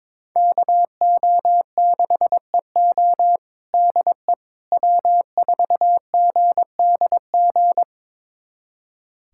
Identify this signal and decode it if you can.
Morse code, heard as a keyed tone, about 22 wpm, 700 Hz. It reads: KO6EO DE W4GDG
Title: USA Callsign DE Callsign - 5 Characters - 22wpm